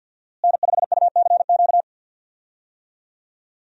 Morse code, heard as a keyed tone, about 48 wpm, 700 Hz. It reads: N5UCQ
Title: USA Call Signs - 5 Characters - Repeated 3x - 50wpm